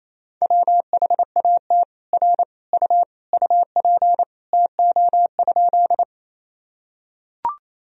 Morse code, heard as a keyed tone, about 28 wpm, 700 Hz. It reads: WHAT R U UP TO?
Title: Sentences from Top 100 Words and Abbreviations 28wpm